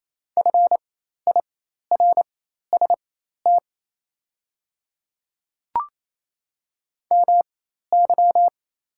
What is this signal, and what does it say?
Morse code, heard as a keyed tone, about 27 wpm, spaced as for 13 wpm, 690 Hz. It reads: FIRST MY
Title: 4x Character Spacing - Top 300 Words - 28wpm